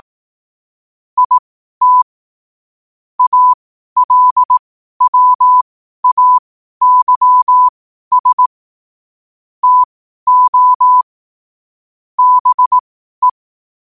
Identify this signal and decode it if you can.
Morse code, heard as a keyed tone, about 18 wpm, spaced as for 12 wpm, 1000 Hz. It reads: IT ALWAYS TO BE
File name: qotdfiles / 12wpm